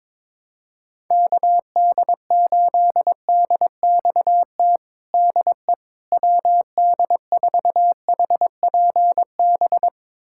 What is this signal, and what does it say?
Morse code, heard as a keyed tone, about 22 wpm, 700 Hz. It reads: KD8DXT DE WD4HPB
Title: USA Callsign DE Callsign - 6 Characters - 22wpm